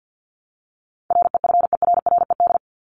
Morse code, heard as a keyed tone, about 50 wpm, 700 Hz. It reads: REFERRED
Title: ED Suffix - Top 100 Words with Base Words 50wpm